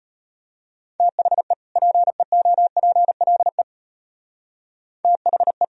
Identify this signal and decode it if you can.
Morse code, heard as a keyed tone, about 31 wpm, 700 Hz. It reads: THE PEOPLE THE
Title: Speed-Racing - Sets of 2 Words 25wpm